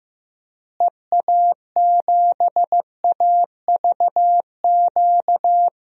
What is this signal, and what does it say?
Morse code, heard as a keyed tone, about 15 wpm, 700 Hz. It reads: EA7AVQ
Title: DXCC-100 Call Signs - 6 Characters - Repeated 3x - 15wpm